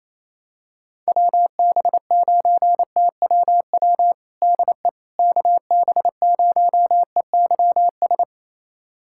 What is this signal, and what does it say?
Morse code, heard as a keyed tone, about 28 wpm, 700 Hz. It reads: WB9TWW DE KB0EYS
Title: USA Callsign DE Callsign - 6 Characters - 28wpm